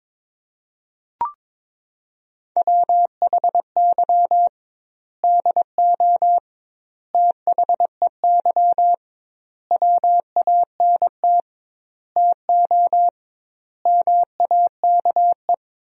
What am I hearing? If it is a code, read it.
Morse code, heard as a keyed tone, about 22 wpm, 700 Hz. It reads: WHY DO THEY WANT TO MAKE
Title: 2x Word-Spacing - Sentences from Top 400 Words - 22wpm